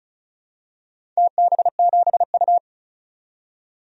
Morse code, heard as a keyed tone, about 34 wpm, 700 Hz. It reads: TB7U
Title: DXCC-100 Call Signs - 4 Characters - 35wpm